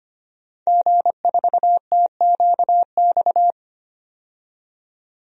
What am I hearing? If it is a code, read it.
Morse code, heard as a keyed tone, about 25 wpm, 700 Hz. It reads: G4TQX
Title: DXCC-100 Call Signs - 5 Characters - 25wpm